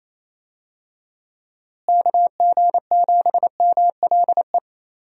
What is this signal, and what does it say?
Morse code, heard as a keyed tone, about 27 wpm, 700 Hz. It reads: KG7MLE
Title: USA Call Signs - 6 Characters - Repeated 2x - 28wpm